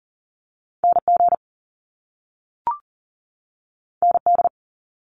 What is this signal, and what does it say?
Morse code, heard as a keyed tone, about 41 wpm, 700 Hz. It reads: NG ND
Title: Top Two Letter Combinations in Words 40wpm